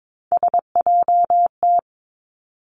Morse code, heard as a keyed tone, about 22 wpm, 700 Hz. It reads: SJT
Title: Three Challenging Letter-Number Combinations 22wpm